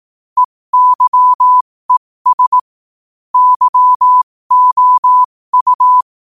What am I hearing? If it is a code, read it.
Morse code, heard as a keyed tone, about 18 wpm, spaced as for 15 wpm, 1000 Hz. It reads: EYES YOU